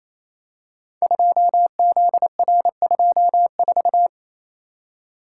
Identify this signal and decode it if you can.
Morse code, heard as a keyed tone, about 28 wpm, 700 Hz. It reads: 2ZR24
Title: Code Groups - 5 Letter:Numbers - 28wpm